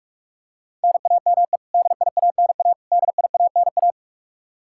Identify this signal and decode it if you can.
Morse code, heard as a keyed tone, about 46 wpm, 700 Hz. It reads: NAME DIANA DIANA
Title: Practice QSO - Exchange 1 - Beginning - 45wpm